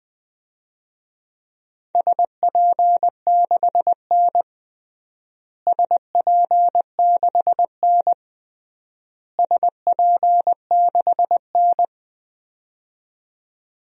Morse code, heard as a keyed tone, about 20 wpm, 700 Hz. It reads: SP6N SP6N SP6N
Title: DXCC-100 Call Signs - 4 Characters - Repeated 3x - 20wpm